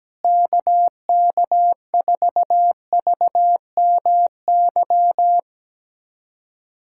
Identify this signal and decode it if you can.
Morse code, heard as a keyed tone, about 17 wpm, 700 Hz. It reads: KK4VMY
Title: USA Callsign DE Callsign 17wpm